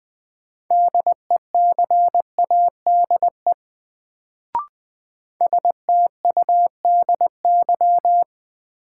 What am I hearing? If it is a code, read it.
Morse code, heard as a keyed tone, about 21 wpm, 700 Hz. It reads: DECADE STUDY
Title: Sets of 5 Words - New Word Drill - 20wpm